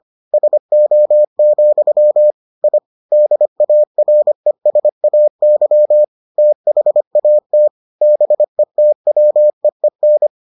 Morse code, heard as a keyed tone, about 25 wpm, 600 Hz. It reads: SO, I DARESAY THAT BETWEEN